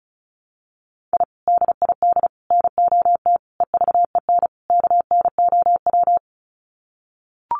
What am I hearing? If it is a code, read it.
Morse code, heard as a keyed tone, about 34 wpm, 700 Hz. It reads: I DID NOT EVEN KNOW
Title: Sets of 5 Words Encore 35wpm